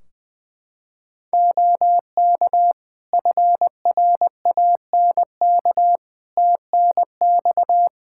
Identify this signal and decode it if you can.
Morse code, heard as a keyed tone, about 20 wpm, 700 Hz. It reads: OK FRANK TNX
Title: Practice QSO - Exchange 4 - Ending - 20wpm